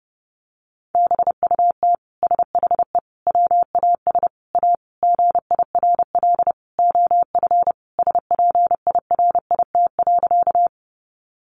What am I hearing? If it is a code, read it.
Morse code, heard as a keyed tone, about 30 wpm, 700 Hz. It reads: BUT SHE WAS A GIRL OF SPIRIT.
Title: Sentences from Top 1400 Words 30wpm